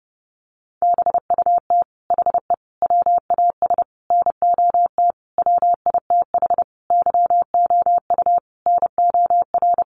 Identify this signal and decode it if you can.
Morse code, heard as a keyed tone, about 30 wpm, 700 Hz. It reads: BUT HE WAS NOT WITH YOU NOR
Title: Sentences from Top 1500 Words 30wpm